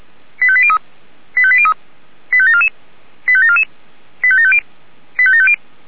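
Five-tone selective call (ZVEI), heard as quick runs of five tones, tones of about 70 ms. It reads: 85803 85803 85744 85744 85750 85750
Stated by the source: ZVEI five-tone selective call received on SDR.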